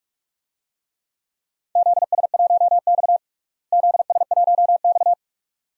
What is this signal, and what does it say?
Morse code, heard as a keyed tone, about 45 wpm, 700 Hz. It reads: ZS1X ZS1X
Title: DXCC-100 Call Signs - 4 Characters - Repeated 2x - 45wpm